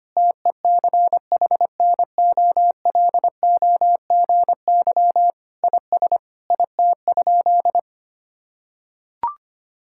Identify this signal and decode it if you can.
Morse code, heard as a keyed tone, about 25 wpm, 700 Hz. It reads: TECHNOLOGY IS IT?
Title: Sentences from Top 1000 Words - Review Part 5 - 25wpm